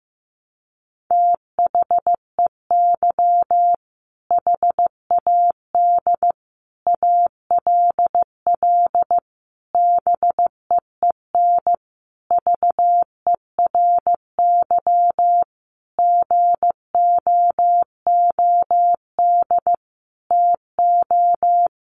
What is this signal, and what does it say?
Morse code, heard as a keyed tone, about 15 wpm, 700 Hz. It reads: THEY HAD ALL BEEN VERY GOOD TO
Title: Sentences from Top 200 Words 15wpm